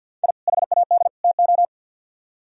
Morse code, heard as a keyed tone, about 61 wpm, 700 Hz. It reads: I HAD TO
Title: Speed-Racing - Sets of 3 Words 50wpm